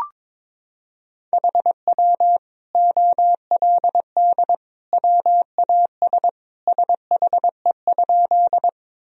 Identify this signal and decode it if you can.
Morse code, heard as a keyed tone, about 22 wpm, 700 Hz. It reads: HW OLD WAS SHE?
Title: Sentences from Top 200 Words and Abbreviations 22wpm